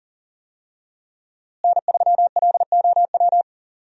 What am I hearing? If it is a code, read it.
Morse code, heard as a keyed tone, about 37 wpm, 700 Hz. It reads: N3LOW
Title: USA Call Signs - 5 Characters - Repeated 3x - 40wpm